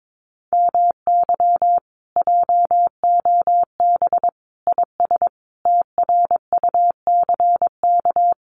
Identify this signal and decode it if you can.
Morse code, heard as a keyed tone, about 22 wpm, 700 Hz. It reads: MY JOB IS TRUCK
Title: Common QSO Phrases Encore 22wpm